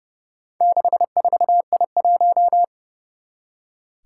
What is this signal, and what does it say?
Morse code, heard as a keyed tone, about 30 wpm, 700 Hz. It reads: 64I1
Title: Four Challenging Letter-Number Combinations 30wpm